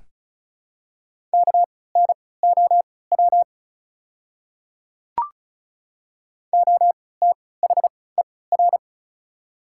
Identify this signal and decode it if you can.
Morse code, heard as a keyed tone, about 36 wpm, spaced as for 20 wpm, 700 Hz. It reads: KNOW OTHER
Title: 3x Character Spacing - Top 500 Words - 35wpm